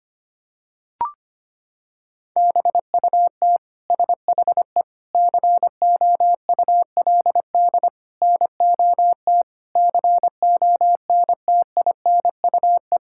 Morse code, heard as a keyed tone, about 25 wpm, 700 Hz. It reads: BUT SHE COULD NOT CONTINUE
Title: Sentences from Top 1000 Words - Review Part 6 - 25wpm